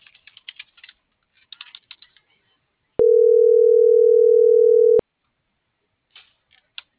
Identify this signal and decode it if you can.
Ringback tone (North America), 440 + 480 Hz.